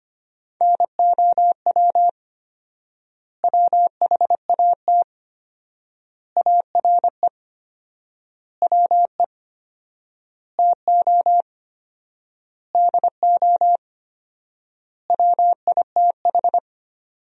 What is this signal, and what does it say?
Morse code, heard as a keyed tone, about 25 wpm, 700 Hz. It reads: NOW WHAT ARE WE TO DO WITH
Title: 4x Word-Spacing - Sentences from Top 100 Words - 25wpm